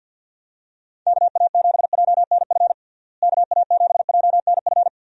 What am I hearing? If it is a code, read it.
Morse code, heard as a keyed tone, about 52 wpm, 700 Hz. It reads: KA7JNF KA7JNF
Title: USA Call Signs - 6 Characters - Repeated 2x - 50wpm